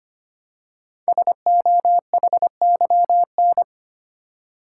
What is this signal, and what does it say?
Morse code, heard as a keyed tone, about 25 wpm, 700 Hz. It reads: SOHYN
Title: Code Groups - 5 Letters - 25wpm